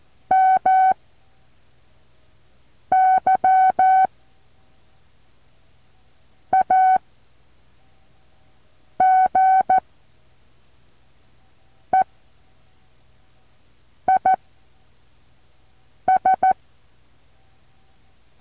Morse code, heard as a keyed tone, about 14 wpm, spaced as for 4 wpm, 760 Hz. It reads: MY AGE IS
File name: intl morse code2